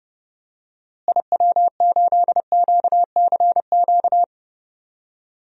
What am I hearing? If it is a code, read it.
Morse code, heard as a keyed tone, about 31 wpm, 700 Hz. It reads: IW8QCQ
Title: DXCC-100 Call Signs - 6 Characters - Repeated 3x - 30wpm